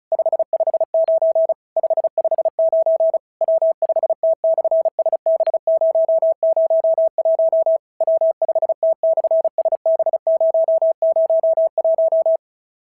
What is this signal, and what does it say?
Morse code, heard as a keyed tone, about 36 wpm, 690 Hz. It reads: 559 559 W5T/SB001 W5T/SB001
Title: SOTA QSO Practice 35wpm